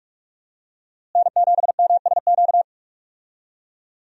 Morse code, heard as a keyed tone, about 45 wpm, 700 Hz. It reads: N7MSQ
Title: USA Call Signs - 5 Characters - Repeated 3x - 45wpm